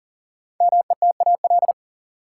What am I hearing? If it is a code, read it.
Morse code, heard as a keyed tone, about 40 wpm, 700 Hz. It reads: METAL
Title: Sentences from Top 1600 Words - New Word Drill Alt - 40wpm